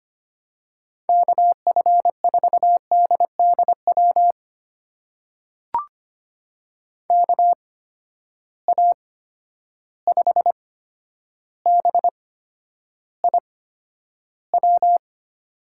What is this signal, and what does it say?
Morse code, heard as a keyed tone, about 25 wpm, spaced as for 11 wpm, 700 Hz. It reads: KF4DDW KA5BIW
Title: USA Call Signs - 6 Characters - 8x Character Spacing - 25wpm